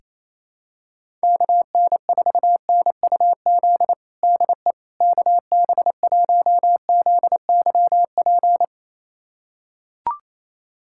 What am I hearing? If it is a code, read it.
Morse code, heard as a keyed tone, about 28 wpm, 700 Hz. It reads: KN4NUZ DE KB1ZYP
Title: USA Callsign DE Callsign 28wpm